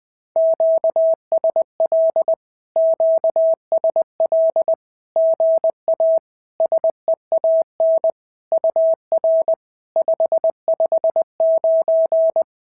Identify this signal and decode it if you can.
Morse code, heard as a keyed tone, about 20 wpm, 690 Hz. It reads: QSL QSL GA SEAN UR 559
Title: SOTA QSO Practice 20wpm